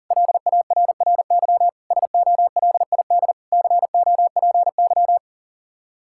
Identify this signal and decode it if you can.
Morse code, heard as a keyed tone, about 37 wpm, 700 Hz. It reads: LARRY SOLID COPY
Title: Practice QSO - Exchange 3 - More Info - 40wpm